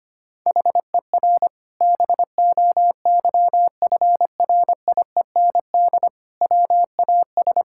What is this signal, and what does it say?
Morse code, heard as a keyed tone, about 25 wpm, 700 Hz. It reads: HER BOYFRIEND WAS
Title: Sentences from Top 1000 Words - Review Part 5 - 25wpm